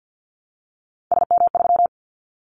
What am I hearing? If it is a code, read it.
Morse code, heard as a keyed tone, about 49 wpm, 700 Hz. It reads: SM3
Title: Callsign Prefixes for Top 25 DXCC - Repeated 3x - 50wpm